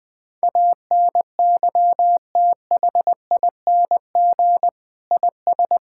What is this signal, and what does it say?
Morse code, heard as a keyed tone, about 19 wpm, 700 Hz. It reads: ANYTHING IS
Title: Sentences from Top 1000 Words - Review Part 6 - 20wpm